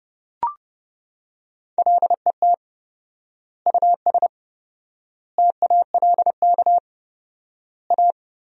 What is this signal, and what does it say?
Morse code, heard as a keyed tone, about 30 wpm, 700 Hz. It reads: LET US TALK A
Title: 4x Word-Spacing - Sentences from Top 400 Words - 30wpm